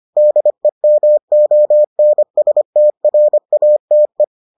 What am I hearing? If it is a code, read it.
Morse code, heard as a keyed tone, about 25 wpm, 600 Hz. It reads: DEMONSTRATE